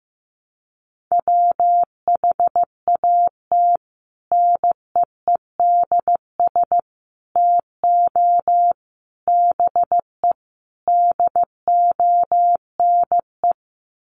Morse code, heard as a keyed tone, about 15 wpm, 700 Hz. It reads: WHAT NEEDS TO BE DONE
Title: Sets of 5 Words Encore 15wpm